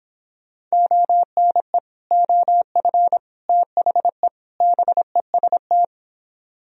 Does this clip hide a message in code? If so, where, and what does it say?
Yes: Morse code from 0.7 to 5.8 s, heard as a keyed tone, about 26 wpm, 700 Hz. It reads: ONE OF THE BEST